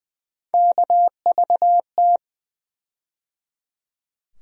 Morse code, heard as a keyed tone, about 20 wpm, 700 Hz. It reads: KVT
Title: USA Call Sign Postfixes - 3 Letters 20wpm